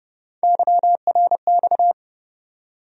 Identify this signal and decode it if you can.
Morse code, heard as a keyed tone, about 30 wpm, 700 Hz. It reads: YRX
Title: USA Call Sign Postfixes - 3 Letters 30wpm